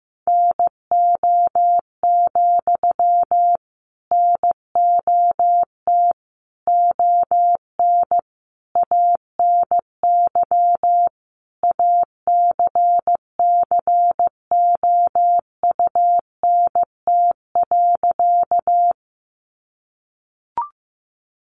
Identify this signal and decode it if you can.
Morse code, heard as a keyed tone, about 15 wpm, 700 Hz. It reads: NO, NOT ON ANY ACCOUNT.
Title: Sentences from Top 900 Words 15wpm